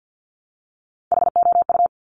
Morse code, heard as a keyed tone, about 50 wpm, 700 Hz. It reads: HOU
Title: Top Three Letter Combinations in Words 50wpm